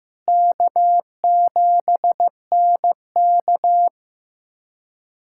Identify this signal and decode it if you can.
Morse code, heard as a keyed tone, about 15 wpm, 700 Hz. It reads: K7NK
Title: USA Callsign DE Callsign - 4 Characters - 15wpm